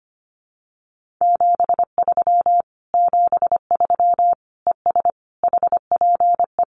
Morse code, heard as a keyed tone, about 25 wpm, 700 Hz. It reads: 73 73 ES HPE
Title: Common QSO Phrases Encore 25wpm